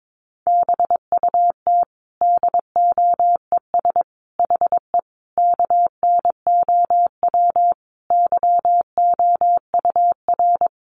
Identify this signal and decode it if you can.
Morse code, heard as a keyed tone, about 22 wpm, 700 Hz. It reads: BUT DOES HE KNOW YOUR
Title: Sentences from Top 1200 Words 22wpm